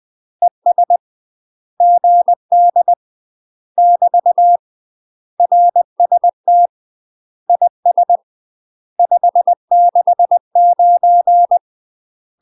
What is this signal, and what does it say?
Morse code, heard as a keyed tone, about 20 wpm, 700 Hz. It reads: ES GD <BT> RST IS 569
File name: UK MASTER QSOs TRAINING / 20WPM / UK MASTER QSOs 0055